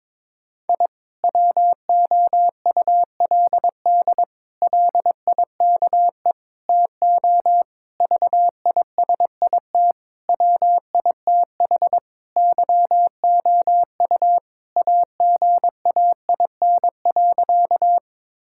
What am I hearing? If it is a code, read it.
Morse code, heard as a keyed tone, about 22 wpm, 700 Hz. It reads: I WOULD LIKE TO VISIT WITH YOU AGAIN.
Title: Sentences from Top 1000 Words - Review Part 6 - 22wpm